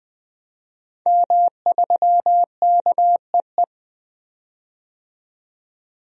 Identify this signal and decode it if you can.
Morse code, heard as a keyed tone, about 21 wpm, 700 Hz. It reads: M3KEE
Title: DXCC-100 Call Signs - 5 Characters - 20wpm